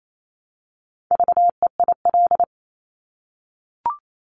Morse code, heard as a keyed tone, about 28 wpm, 700 Hz. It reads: VEIL
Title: Multi-letter Phonemes Repeated 3x 28wpm